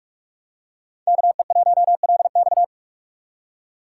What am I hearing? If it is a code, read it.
Morse code, heard as a keyed tone, about 46 wpm, 700 Hz. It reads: KE1LX
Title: USA Call Signs - 5 Characters - 4x Character Spacing - 45wpm